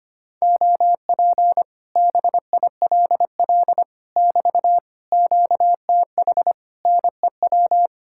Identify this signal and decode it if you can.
Morse code, heard as a keyed tone, about 24 wpm, 700 Hz. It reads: OP BILL <BT> QTH NEW
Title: Speed-Racing Common QSO Phrases Encore - 25wpm